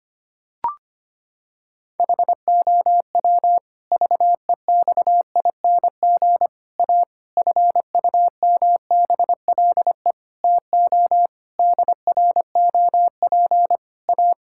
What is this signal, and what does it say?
Morse code, heard as a keyed tone, about 25 wpm, 700 Hz. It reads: HOW VEXING A FUMBLE TO DROP A
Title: Pangram Challenge v2 - 25wpm